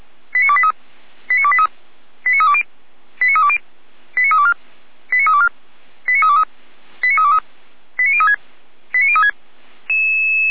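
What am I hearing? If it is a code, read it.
Five-tone selective call (ZVEI), heard as quick runs of five tones, tones of about 70 ms. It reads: 89283 89283 89320 89320 89324 89324 89323 89323 89036 89036
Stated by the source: ZVEI five-tone call 89508 received on SDR, recorded by fire brigade Feuerwehr Orlamünde.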